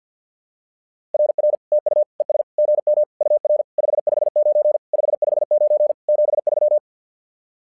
Morse code, heard as a keyed tone, about 50 wpm, 600 Hz. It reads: RR TU ES GM UR 559 559 73
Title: SOTA QSO Practice 50wpm